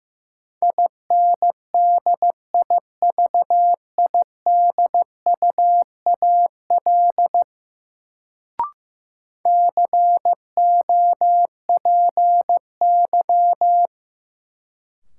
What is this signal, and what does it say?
Morse code, heard as a keyed tone, about 15 wpm, 700 Hz. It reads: INDIVIDUAL COPY
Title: Sentences from Top 1400 Words - New Word Drill Alt - 15wpm